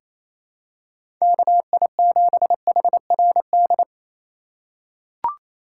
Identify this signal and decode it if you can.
Morse code, heard as a keyed tone, about 28 wpm, 700 Hz. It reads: KI7HRD
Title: USA Call Signs - 6 Characters - 3x Character Spacing - 28wpm